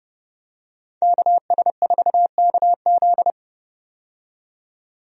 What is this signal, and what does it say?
Morse code, heard as a keyed tone, about 32 wpm, 700 Hz. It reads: KS4KZ
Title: USA Call Signs - 5 Characters - 30wpm